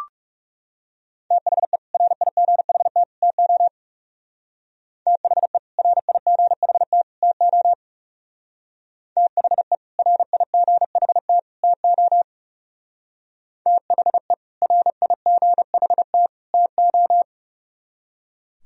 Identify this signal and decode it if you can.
Morse code, heard as a keyed tone, about 36 wpm, 700 Hz. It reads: THE RIGHT TO THE RIGHT TO THE RIGHT TO THE RIGHT TO
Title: Speed-Racing - Sets of 3 Words 30wpm